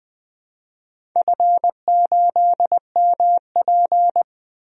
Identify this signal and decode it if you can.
Morse code, heard as a keyed tone, about 20 wpm, 700 Hz. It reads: F8MP
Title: DXCC-100 Call Signs - 4 Characters - 20wpm